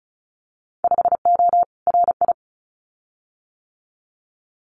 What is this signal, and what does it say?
Morse code, heard as a keyed tone, about 36 wpm, 700 Hz. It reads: 5O RI
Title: Winter Field Day Exchanges Encore 35wpm